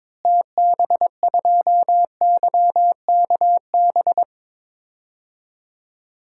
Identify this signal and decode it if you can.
Morse code, heard as a keyed tone, about 22 wpm, 700 Hz. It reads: TB2YKB
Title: DXCC-100 Call Signs - 6 Characters - Repeated 2x - 22wpm